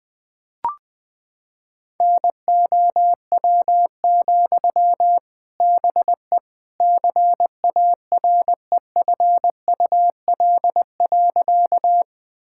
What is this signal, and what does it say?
Morse code, heard as a keyed tone, about 20 wpm, 700 Hz. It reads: NOW, BE CAREFUL.
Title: Sentences from Top 1600 Words 20wpm